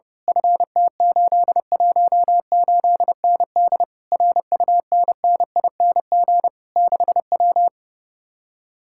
Morse code, heard as a keyed tone, about 30 wpm, 700 Hz. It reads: FT818ND RUNNING 6W
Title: Speed-Racing Common QSO Phrases - 30wpm